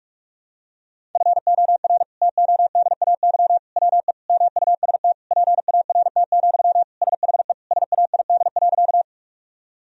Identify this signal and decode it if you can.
Morse code, heard as a keyed tone, about 46 wpm, 700 Hz. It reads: FOR TODAY WE MUST PART, SHE SAID.
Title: Sentences from Top 1000 Words - Review Part 1 - 45wpm